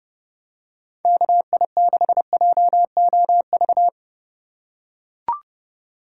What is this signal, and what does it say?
Morse code, heard as a keyed tone, about 30 wpm, 700 Hz. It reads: KI6JOV